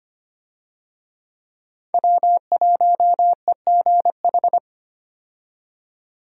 Morse code, heard as a keyed tone, about 25 wpm, 700 Hz. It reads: W1EGH
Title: USA Call Signs - 5 Characters - 25wpm